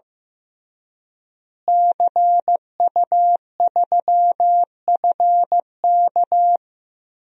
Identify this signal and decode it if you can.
Morse code, heard as a keyed tone, about 15 wpm, 700 Hz. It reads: CU3FK
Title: DXCC-100 Call Signs - 5 Characters - Repeated 3x - 15wpm